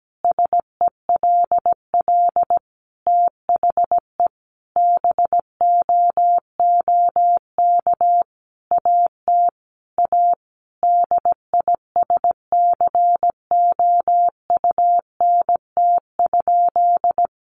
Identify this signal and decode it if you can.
Morse code, heard as a keyed tone, about 17 wpm, 700 Hz. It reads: SELL THE BOOK AT A DISCOUNT?
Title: Sentences from Top 1200 Words 17wpm